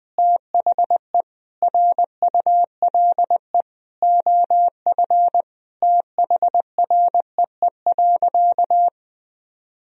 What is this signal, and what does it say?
Morse code, heard as a keyed tone, about 21 wpm, 700 Hz. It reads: THE RULE OF THREE.
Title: Sentences from Top 1000 Words - Review Part 2 - 20wpm